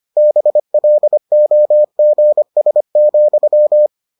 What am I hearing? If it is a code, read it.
Morse code, heard as a keyed tone, about 25 wpm, 600 Hz. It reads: BLOGS,